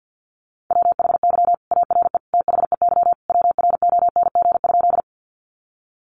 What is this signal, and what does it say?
Morse code, heard as a keyed tone, about 48 wpm, 700 Hz. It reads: WHY ARE THEY WRONG?
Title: Sentences from Top 900 Words 50wpm